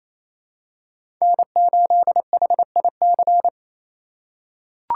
Morse code, heard as a keyed tone, about 28 wpm, 700 Hz. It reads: N8HIC
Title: USA Call Signs - 8x Character Spacing - 28wpm